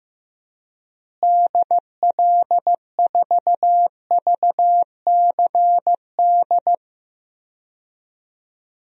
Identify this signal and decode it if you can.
Morse code, heard as a keyed tone, about 15 wpm, 700 Hz. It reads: DL4VCD
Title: DXCC-100 Call Signs - 6 Characters - Repeated 2x - 15wpm